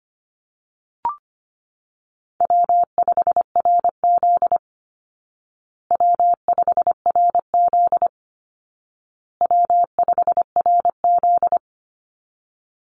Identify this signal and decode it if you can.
Morse code, heard as a keyed tone, about 25 wpm, 700 Hz. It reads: W5RZ W5RZ W5RZ
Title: Easy Call Signs Repeated 3x - 25wpm